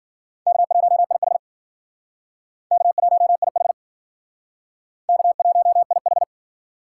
Morse code, heard as a keyed tone, about 50 wpm, 700 Hz. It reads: K1IH K1IH K1IH
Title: Speed Racing - USA 1x2 Call Signs 40wpm